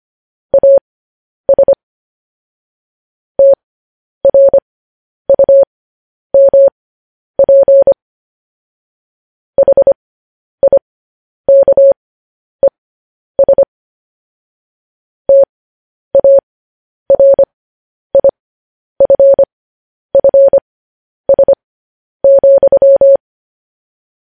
Morse code, heard as a keyed tone, about 25 wpm, spaced as for 10 wpm, 550 Hz. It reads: AS TRUMP HIKES TARIFFS,
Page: news headlines in morse code at 10 wpm